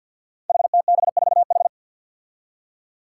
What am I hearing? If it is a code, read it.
Morse code, heard as a keyed tone, about 52 wpm, 700 Hz. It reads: HTB4H
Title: Five Challenging Letter-Number Combinations 50wpm